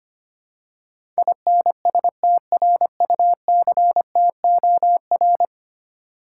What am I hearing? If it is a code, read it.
Morse code, heard as a keyed tone, about 25 wpm, 700 Hz. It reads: INSTRUCTOR
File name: Ten Letter Words - 25wpm